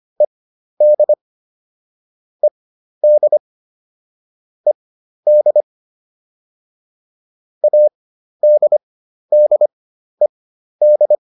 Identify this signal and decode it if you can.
Morse code, heard as a keyed tone, about 25 wpm, spaced as for 12 wpm, 600 Hz. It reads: ED ED ED ADDED